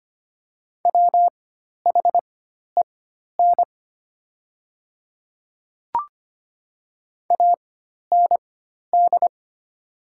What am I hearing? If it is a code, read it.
Morse code, heard as a keyed tone, about 25 wpm, spaced as for 12 wpm, 700 Hz. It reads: WHEN AND
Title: 4x Character Spacing - Top 100 Words - 25wpm